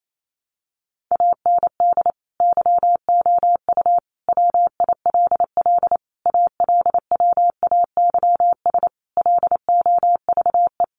Morse code, heard as a keyed tone, about 28 wpm, 700 Hz. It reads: AND YOU WILL ALWAYS LOVE
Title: Sentences from Top 400 Words 28wpm